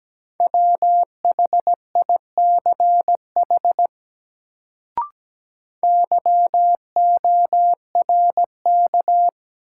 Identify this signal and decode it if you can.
Morse code, heard as a keyed tone, about 17 wpm, 700 Hz. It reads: WHICH YORK
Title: Sets of 2 Words - New Word Drill - 17wpm